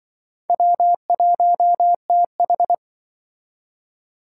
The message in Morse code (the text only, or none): W1TH